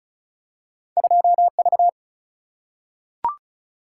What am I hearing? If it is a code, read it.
Morse code, heard as a keyed tone, about 36 wpm, 700 Hz. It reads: 2 V
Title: Single-Letter Single-Number 35wpm